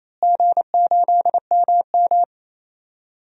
Morse code, heard as a keyed tone, about 28 wpm, 700 Hz. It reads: G8MM
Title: DXCC-100 Call Signs - 4 Characters - Repeated 2x - 28wpm